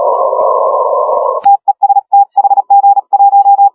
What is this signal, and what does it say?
Morse code, heard as a keyed tone, about 38 wpm, 810 Hz, with radio noise. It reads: TEST HG1
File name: HG1R CW SSB MIX